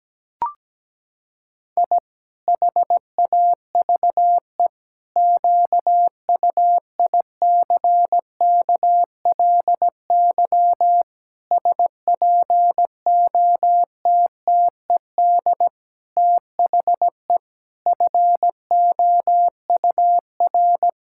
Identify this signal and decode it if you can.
Morse code, heard as a keyed tone, about 17 wpm, 700 Hz. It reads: I HAVE QUICKLY SPOTTED THE FOUR
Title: Pangram Challenge v2 - 17wpm